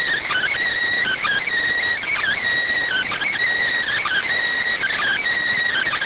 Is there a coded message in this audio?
Yes: Emergency Alert System (SAME) tones (header bursts) from 1.0 to 5.2 s, heard as harsh buzzing bursts.